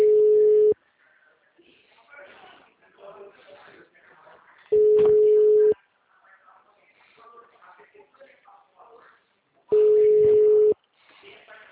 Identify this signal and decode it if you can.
Ringback tone (Europe, 425 Hz), with speech.